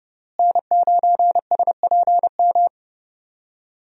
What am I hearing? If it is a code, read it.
Morse code, heard as a keyed tone, about 31 wpm, 700 Hz. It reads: N9SPM
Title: USA Call Signs - 5 Characters - Repeated 2x - 30wpm